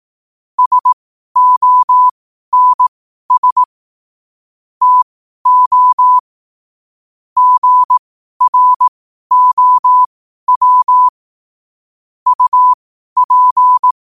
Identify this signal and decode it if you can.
Morse code, heard as a keyed tone, about 18 wpm, spaced as for 12 wpm, 1000 Hz. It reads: SONS TO GROW UP
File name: qotdfiles / 12wpm